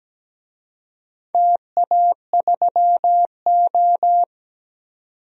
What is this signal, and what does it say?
Morse code, heard as a keyed tone, about 17 wpm, 700 Hz. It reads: TA3O
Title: DXCC-100 Call Signs - 4 Characters - 17wpm